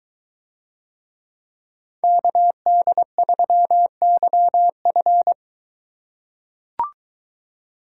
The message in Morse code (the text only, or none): KD3YF